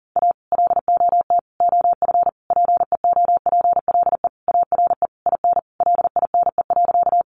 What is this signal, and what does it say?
Morse code, heard as a keyed tone, about 40 wpm, 700 Hz. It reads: A LOT OF PEOPLE ARE IN LINE.
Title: Additional Easy Phrases 40wpm